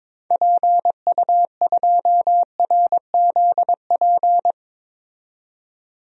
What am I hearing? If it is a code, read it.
Morse code, heard as a keyed tone, about 22 wpm, 700 Hz. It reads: PU2RZP
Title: DXCC-100 Call Signs - 6 Characters - 22wpm